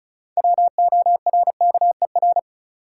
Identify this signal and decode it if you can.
Morse code, heard as a keyed tone, about 36 wpm, 700 Hz. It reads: WORKER
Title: Six Letter Words - 35wpm